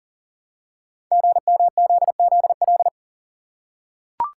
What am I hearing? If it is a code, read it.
Morse code, heard as a keyed tone, about 39 wpm, 690 Hz. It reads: GMZZL
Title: Code Groups - 5 Letters - 40wpm